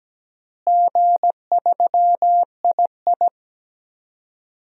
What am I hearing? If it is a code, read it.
Morse code, heard as a keyed tone, about 17 wpm, 700 Hz. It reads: G3II
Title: DXCC-100 Call Signs - 4 Characters - Repeated 2x - 17wpm